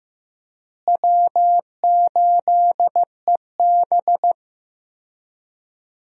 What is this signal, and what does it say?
Morse code, heard as a keyed tone, about 15 wpm, 700 Hz. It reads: W8EB
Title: USA Call Signs - 4 Characters - Repeated 2x - 15wpm